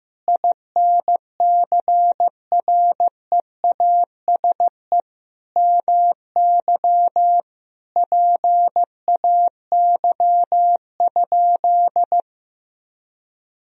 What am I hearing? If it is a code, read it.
Morse code, heard as a keyed tone, about 16 wpm, 700 Hz. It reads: INCREASE MY PAY?
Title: Sentences from Top 1000 Words - Review Part 4 - 15wpm